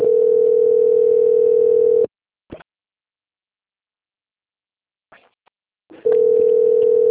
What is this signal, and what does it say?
Ringback tone (North America), 440 + 480 Hz.